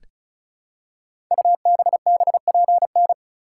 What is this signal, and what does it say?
Morse code, heard as a keyed tone, about 36 wpm, 690 Hz. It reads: UBBPN
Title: Code Groups - 5 Letter:Numbers - 35wpm